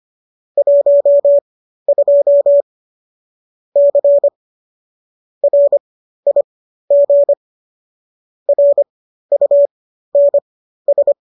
Morse code, heard as a keyed tone, about 25 wpm, spaced as for 13 wpm, 570 Hz. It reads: 12 C RIG RUNS